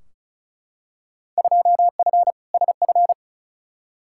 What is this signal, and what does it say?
Morse code, heard as a keyed tone, about 36 wpm, 700 Hz. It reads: 2F SF
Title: Field Day Exchanges - 35wpm